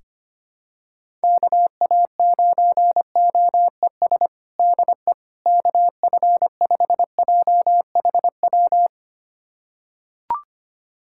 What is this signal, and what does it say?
Morse code, heard as a keyed tone, about 25 wpm, 700 Hz. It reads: KA9OES DE KF5JHW
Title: USA Callsign DE Callsign 25wpm